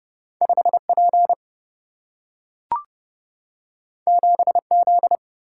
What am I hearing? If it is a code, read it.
Morse code, heard as a keyed tone, about 30 wpm, 700 Hz. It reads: 5 P 7 Z
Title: Single-Letter Single-Number 30wpm